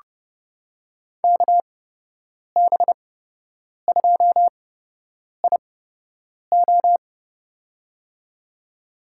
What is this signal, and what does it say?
Morse code, heard as a keyed tone, about 30 wpm, spaced as for 8 wpm, 700 Hz. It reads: KB2IO
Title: USA Call Signs - 5 Characters - 8x Character Spacing - 30wpm